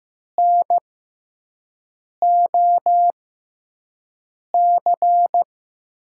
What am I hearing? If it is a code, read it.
Morse code, heard as a keyed tone, about 15 wpm, spaced as for 5 wpm, 700 Hz. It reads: NOC
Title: USA Call Signs - 5 Characters - 6x Character Spacing - 15wpm